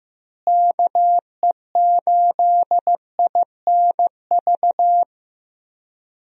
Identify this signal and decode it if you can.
Morse code, heard as a keyed tone, about 16 wpm, 700 Hz. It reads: KE8INV
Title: USA Call Signs - 6 Characters - Repeated 3x - 15wpm